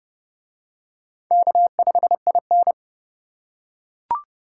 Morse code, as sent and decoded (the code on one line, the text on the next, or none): -.- ..... .. -.
K5IN